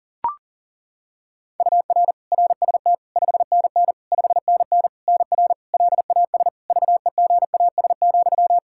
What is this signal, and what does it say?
Morse code, heard as a keyed tone, about 40 wpm, 700 Hz. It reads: UR RST 5NN 5NN NR LAS VEGAS,
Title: K1USN SST QSO Practice 40wpm